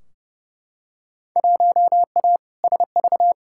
Morse code, heard as a keyed tone, about 31 wpm, 700 Hz. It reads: 1A SV
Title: Field Day Exchanges - 30wpm